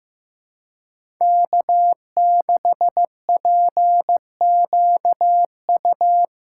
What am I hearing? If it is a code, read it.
Morse code, heard as a keyed tone, about 15 wpm, 700 Hz. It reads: K6PQU
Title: USA Callsign DE Callsign - 5 Characters - 15wpm